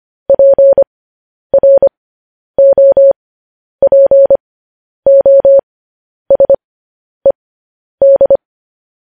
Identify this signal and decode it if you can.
Morse code, heard as a keyed tone, about 25 wpm, spaced as for 10 wpm, 550 Hz. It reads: PROPOSED
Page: news headlines in morse code at 10 wpm